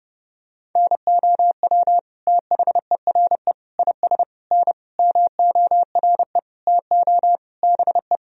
Morse code, heard as a keyed tone, about 30 wpm, 700 Hz. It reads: NOW THERE IS N MORE TO BE
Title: Sentences from Top 100 Words and Abbreviations 30wpm